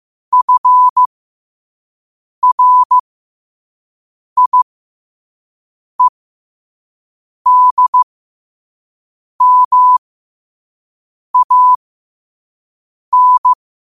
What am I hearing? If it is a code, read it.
Morse code, heard as a keyed tone, about 15 wpm, spaced as for 5 wpm, 1000 Hz. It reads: FRIEDMAN
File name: qotdfiles / 5wpm